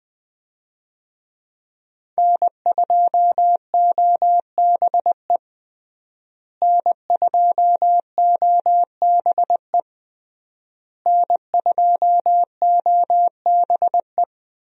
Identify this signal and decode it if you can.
Morse code, heard as a keyed tone, about 20 wpm, 700 Hz. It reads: N2OBE N2OBE N2OBE
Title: USA Call Signs - 5 Characters - Repeated 3x - 20wpm